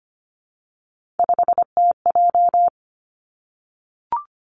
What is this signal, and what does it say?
Morse code, heard as a keyed tone, about 25 wpm, 700 Hz. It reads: 5TJ
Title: Three Challenging Letter-Number Combinations 25wpm